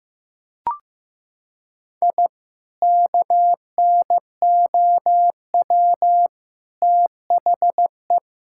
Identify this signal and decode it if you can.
Morse code, heard as a keyed tone, about 15 wpm, 700 Hz. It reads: I KNOW THE
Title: Sentences from Top 1000 Words - Review Part 5 - 15wpm